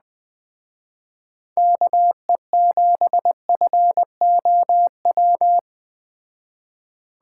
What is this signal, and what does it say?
Morse code, heard as a keyed tone, about 21 wpm, 700 Hz. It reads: KE7FOW